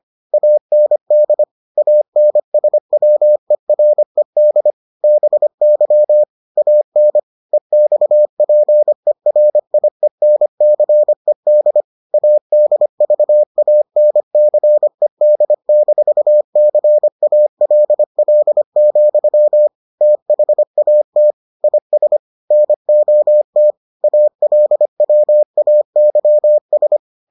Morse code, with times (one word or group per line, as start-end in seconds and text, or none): 0.3-1.4 AND
1.8-4.7 ANSWERED
5.0-6.2 BY
6.6-7.2 AN
7.5-11.8 EXPERIENCED
12.1-19.7 ADVANCED-CALL,
20.0-21.3 THAT
21.6-22.2 IS
22.5-23.7 NOT
24.0-27.0 ALWAYS